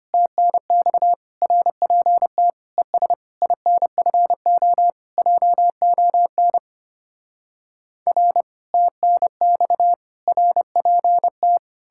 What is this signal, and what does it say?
Morse code, heard as a keyed tone, about 33 wpm, 700 Hz. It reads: TNX RPT ES INFO JON R TNX RPT